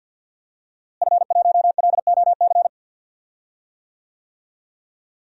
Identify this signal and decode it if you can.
Morse code, heard as a keyed tone, about 50 wpm, 700 Hz. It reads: F1LOC
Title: DXCC-100 Call Signs - 5 Characters - Repeated 3x - 50wpm